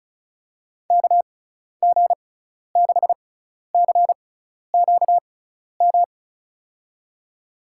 Morse code, heard as a keyed tone, about 34 wpm, spaced as for 12 wpm, 690 Hz. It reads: KG6CQM
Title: USA Call Signs - 6 Characters - 6x Character Spacing - 35wpm